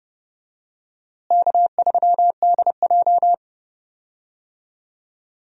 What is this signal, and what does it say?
Morse code, heard as a keyed tone, about 30 wpm, 700 Hz. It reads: K3DJ
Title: USA Call Signs - 4 Characters - Repeated 2x - 30wpm